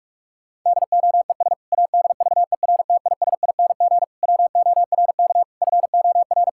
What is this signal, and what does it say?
Morse code, heard as a keyed tone, about 46 wpm, 700 Hz. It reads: DOES ADVERTISING WORK FOR
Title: Sentences from Top 1000 Words - Review Part 6 - 45wpm